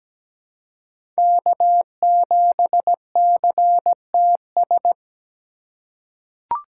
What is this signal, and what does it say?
Morse code, heard as a keyed tone, about 17 wpm, 700 Hz. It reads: K7CTS
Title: USA Call Signs - 5 Characters - 2x Character Spacing - 17wpm